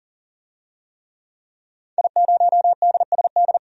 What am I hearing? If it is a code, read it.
Morse code, heard as a keyed tone, about 40 wpm, 700 Hz. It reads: I0DSD
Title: DXCC-100 Call Signs - 5 Characters - 40wpm